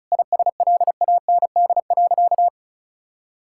Morse code, heard as a keyed tone, about 34 wpm, 700 Hz. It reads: ISLAND.
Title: Sentences from Top 1000 Words - Review Part 5 - 35wpm